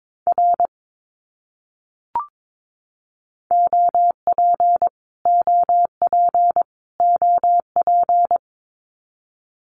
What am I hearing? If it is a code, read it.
Morse code, heard as a keyed tone, about 22 wpm, 700 Hz. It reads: R OP OP OP
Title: Abbreviations Repeated 3x - 22wpm